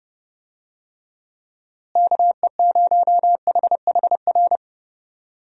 Morse code, heard as a keyed tone, about 30 wpm, 700 Hz. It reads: KE0HHR
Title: USA Call Signs - 6 Characters - Repeated 3x - 30wpm